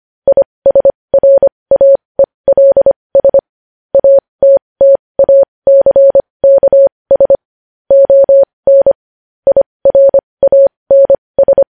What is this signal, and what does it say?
Morse code, heard as a keyed tone, about 25 wpm, spaced as for 20 wpm, 550 Hz. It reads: ISRAELS ATTACKS ON IRANS
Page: news-headlines-in-morse-code-at-20-wpm